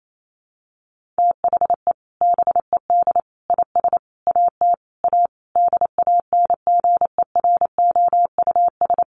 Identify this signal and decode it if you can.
Morse code, heard as a keyed tone, about 28 wpm, 700 Hz. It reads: THE BED IS AT A DANGEROUS
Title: Sentences from Top 1500 Words 28wpm